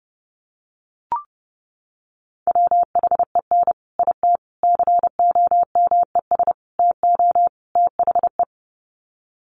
Morse code, heard as a keyed tone, about 30 wpm, 700 Hz. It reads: WHEN IT COMES TO THE
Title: Sets of 5 Words 30wpm